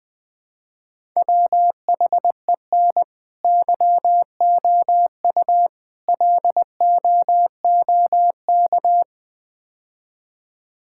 Morse code, heard as a keyed tone, about 20 wpm, 700 Hz. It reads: WHEN YOU LOOK
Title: Build A Phrase - 5-Word Phrases - 20wpm